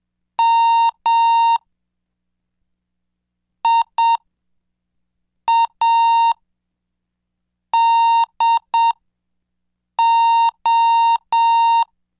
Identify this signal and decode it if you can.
Morse code, heard as a keyed tone, about 7 wpm, spaced as for 4 wpm, 920 Hz. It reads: MIADO